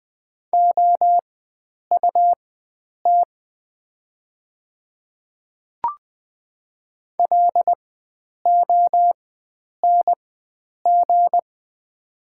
Morse code, heard as a keyed tone, about 20 wpm, spaced as for 9 wpm, 700 Hz. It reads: OUT LONG
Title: 4x Character Spacing - Top 100 Words - 20wpm